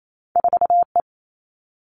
Morse code, heard as a keyed tone, about 28 wpm, 700 Hz. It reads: 4 E
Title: Two Challenging Letter-Number Combinations 28wpm